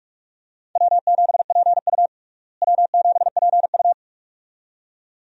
Morse code, heard as a keyed tone, about 46 wpm, 700 Hz. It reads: W7PU W7PU
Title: USA Call Signs - 4 Characters - Repeated 2x - 45wpm